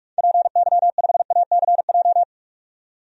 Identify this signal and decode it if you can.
Morse code, heard as a keyed tone, about 42 wpm, 700 Hz. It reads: PY5ACJ
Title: DXCC-100 Call Signs - 6 Characters - 45wpm